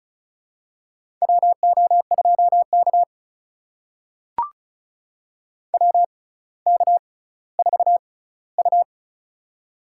Morse code, heard as a keyed tone, about 36 wpm, spaced as for 18 wpm, 700 Hz. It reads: WO2K WK4U
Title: USA Call Signs - 6x Character Spacing - 35wpm